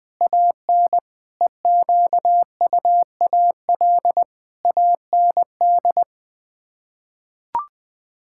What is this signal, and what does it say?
Morse code, heard as a keyed tone, about 20 wpm, 700 Hz. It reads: AN EQUAL AND
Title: Sets of 8 Words - 20wpm